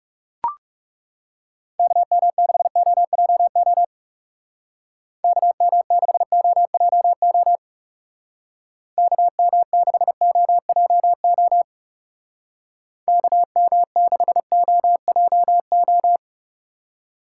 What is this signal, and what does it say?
Morse code, heard as a keyed tone, about 38 wpm, 700 Hz. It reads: KM6OJO KM6OJO KM6OJO KM6OJO